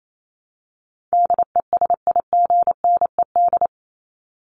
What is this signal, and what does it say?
Morse code, heard as a keyed tone, about 28 wpm, 700 Hz. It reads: DESIGNED
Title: ED Suffix - Top 100 Words with Base Words 28wpm